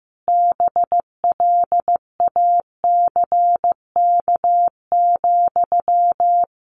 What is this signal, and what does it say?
Morse code, heard as a keyed tone, about 15 wpm, 700 Hz. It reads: BLACK,
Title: Sentences from Top 1500 Words 15wpm